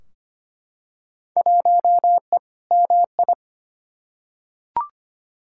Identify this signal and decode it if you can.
Morse code, heard as a keyed tone, about 25 wpm, 700 Hz. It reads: 1E MI
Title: Field Day Exchanges - 25wpm